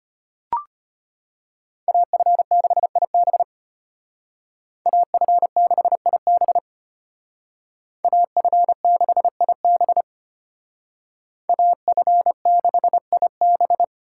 Morse code, heard as a keyed tone, about 33 wpm, 700 Hz. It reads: AF6IB AF6IB AF6IB AF6IB